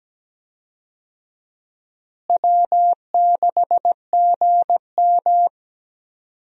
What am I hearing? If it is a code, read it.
Morse code, heard as a keyed tone, about 17 wpm, 700 Hz. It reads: W6GM
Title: USA Call Signs - 4 Characters - Repeated 2x - 17wpm